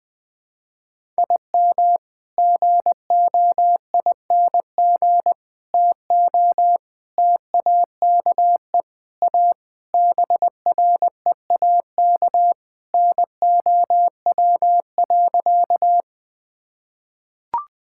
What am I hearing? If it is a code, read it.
Morse code, heard as a keyed tone, about 20 wpm, 700 Hz. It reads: IM GOING TO TAKE A BREAK NOW.
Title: Sentences with 5-Word Phrases - 20wpm